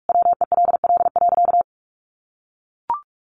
Morse code, heard as a keyed tone, about 44 wpm, 700 Hz. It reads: WELL.
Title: Sentences from Top 800 Words 45wpm